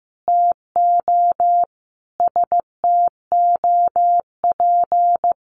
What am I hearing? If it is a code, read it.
Morse code, heard as a keyed tone, about 15 wpm, 700 Hz. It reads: TO STOP
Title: Sentences from Top 1500 Words 15wpm